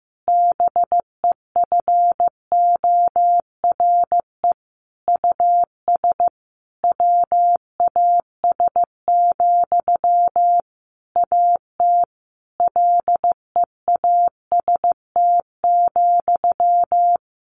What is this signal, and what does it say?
Morse code, heard as a keyed tone, about 15 wpm, 700 Hz. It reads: BEFORE US WAS, AT LEAST,
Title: Sentences from Top 1400 Words 15wpm